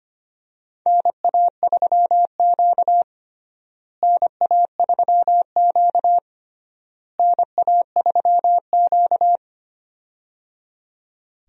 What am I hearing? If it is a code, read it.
Morse code, heard as a keyed tone, about 25 wpm, 700 Hz. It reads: NA3Q NA3Q NA3Q
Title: USA Call Signs - 4 Characters - Repeated 3x - 25wpm